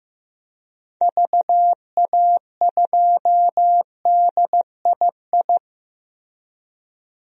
Morse code, heard as a keyed tone, about 15 wpm, 700 Hz. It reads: VA2DII
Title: DXCC-100 Call Signs - 6 Characters - Repeated 2x - 15wpm